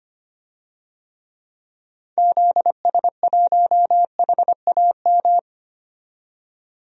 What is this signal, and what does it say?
Morse code, heard as a keyed tone, about 25 wpm, 700 Hz. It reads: ZS1HAM
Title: DXCC-100 Call Signs - 6 Characters - 25wpm